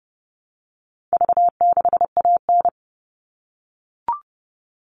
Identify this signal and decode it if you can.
Morse code, heard as a keyed tone, about 30 wpm, 700 Hz. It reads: V6AN
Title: All Cut Numbers - 4-Digit 30wpm